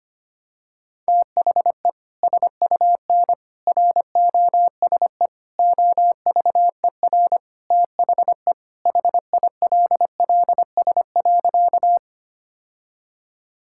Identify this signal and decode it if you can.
Morse code, heard as a keyed tone, about 25 wpm, 700 Hz. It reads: THE SUN ROSE OVER THE HILLS.
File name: practice / 25wpm / Sentences with 2-Word Phrases - 25wpm